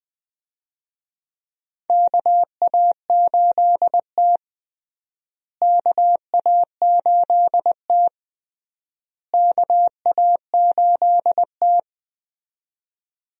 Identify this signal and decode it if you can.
Morse code, heard as a keyed tone, about 20 wpm, 700 Hz. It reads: KA8T KA8T KA8T
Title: USA Call Signs - 4 Characters - Repeated 3x - 20wpm